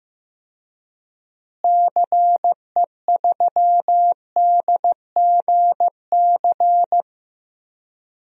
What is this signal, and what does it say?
Morse code, heard as a keyed tone, about 16 wpm, 700 Hz. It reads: CE3DGC
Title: DXCC-100 Call Signs - 6 Characters - Repeated 2x - 15wpm